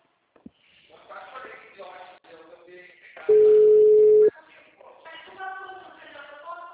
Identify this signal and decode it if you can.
Ringback tone (Europe, 425 Hz), with speech.